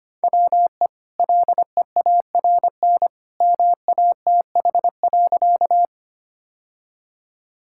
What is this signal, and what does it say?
Morse code, heard as a keyed tone, about 25 wpm, 700 Hz. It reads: WE LEARN MATH.
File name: Easy Sentences with Hints - 25wpm